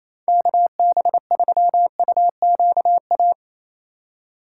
Morse code, heard as a keyed tone, about 37 wpm, 700 Hz. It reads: KB3UQA